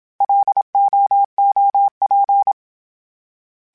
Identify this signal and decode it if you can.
Morse code, heard as a keyed tone, about 27 wpm, 800 Hz. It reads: LOOP
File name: fromSMs / FromSMs mp3s / Antennas@25